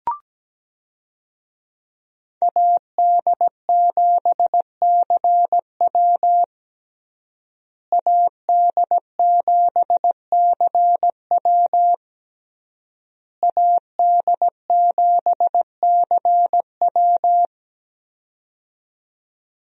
Morse code, heard as a keyed tone, about 17 wpm, 700 Hz. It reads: AD7CW AD7CW AD7CW
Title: USA Call Signs - 5 Characters - Repeated 3x - 17wpm